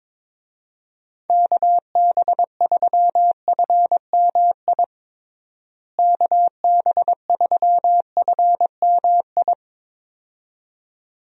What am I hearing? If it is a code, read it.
Morse code, heard as a keyed tone, about 22 wpm, 700 Hz. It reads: KB3FMI KB3FMI
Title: USA Call Signs - 6 Characters - Repeated 2x - 22wpm